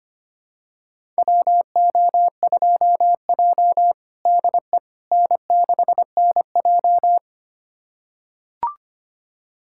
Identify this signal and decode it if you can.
Morse code, heard as a keyed tone, about 25 wpm, 700 Hz. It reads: WO2J DE N6NJ
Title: USA Callsign DE Callsign - 4 Characters - 25wpm